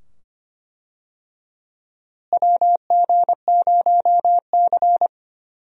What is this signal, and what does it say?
Morse code, heard as a keyed tone, about 25 wpm, 700 Hz. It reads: WG0C
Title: USA Call Signs - 4 Characters - Repeated 2x - 25wpm